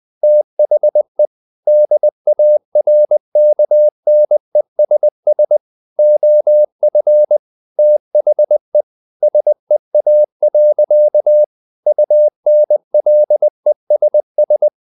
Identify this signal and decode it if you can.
Morse code, heard as a keyed tone, about 20 wpm, 600 Hz. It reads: THE DARKNESS OF THE SEA. UNLESS
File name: the old man and the sea / the old man and the sea part 1 20wpm